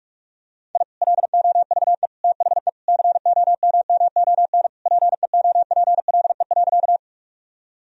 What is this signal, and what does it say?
Morse code, heard as a keyed tone, about 43 wpm, 700 Hz. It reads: I LOVE THE COMMON PEOPLE.
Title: Sentences from Top 1000 Words - Review Part 3 - 45wpm